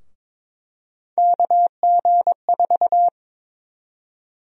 Morse code, heard as a keyed tone, about 21 wpm, 700 Hz. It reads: KG4
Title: USA Call Sign Prefixes - 22wpm